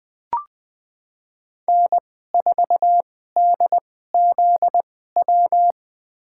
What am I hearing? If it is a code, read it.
Morse code, heard as a keyed tone, about 21 wpm, spaced as for 15 wpm, 700 Hz. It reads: N4DZW
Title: USA Call Signs - 5 Characters - 2x Character Spacing - 20wpm